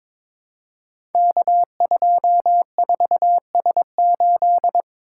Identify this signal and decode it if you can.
Morse code, heard as a keyed tone, about 23 wpm, 700 Hz. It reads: K24S8
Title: Code Groups - 5 Letter:Numbers - 22wpm